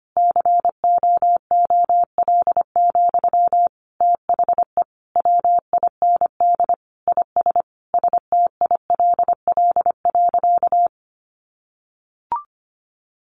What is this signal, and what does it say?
Morse code, heard as a keyed tone, about 25 wpm, 700 Hz. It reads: COOL, THE WIND IS STILL.
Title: Sentences from Top 1200 Words 25wpm